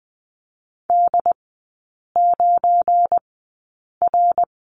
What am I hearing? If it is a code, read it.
Morse code, heard as a keyed tone, about 20 wpm, spaced as for 8 wpm, 700 Hz. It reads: D9R
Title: 172 Lesson 28e - Number 9 - Three Characters 20wpm